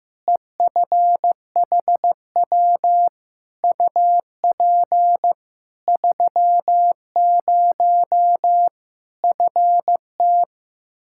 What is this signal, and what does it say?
Morse code, heard as a keyed tone, about 16 wpm, 700 Hz. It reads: EFHW UP 30 FT
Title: Practice QSO - Exchange 2 - Info - 15wpm